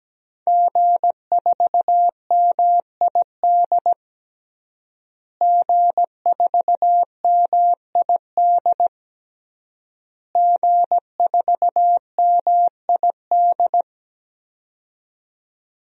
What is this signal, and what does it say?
Morse code, heard as a keyed tone, about 17 wpm, 700 Hz. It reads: G4MID G4MID G4MID
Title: DXCC-100 Call Signs - 5 Characters - Repeated 3x - 17wpm